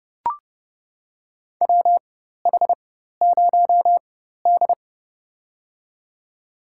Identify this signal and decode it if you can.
Morse code, heard as a keyed tone, about 30 wpm, spaced as for 14 wpm, 700 Hz. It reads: WH0D
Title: USA Call Signs - 4x Character Spacing - 30wpm